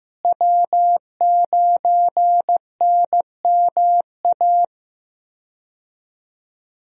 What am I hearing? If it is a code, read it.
Morse code, heard as a keyed tone, about 15 wpm, 700 Hz. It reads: W9NMA
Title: USA Call Signs - 5 Characters - Repeated 2x - 15wpm